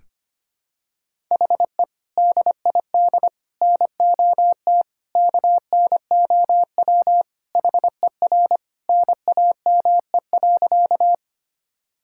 Morse code, heard as a keyed tone, about 25 wpm, 700 Hz. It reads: HE DID NOT KNOW HER NAME.
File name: practice / 25wpm / Sentences with 4-Word Phrases - 25wpm